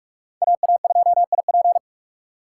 Morse code, heard as a keyed tone, about 42 wpm, 700 Hz. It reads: AA2IP
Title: USA Call Signs - 5 Characters - 4x Character Spacing - 45wpm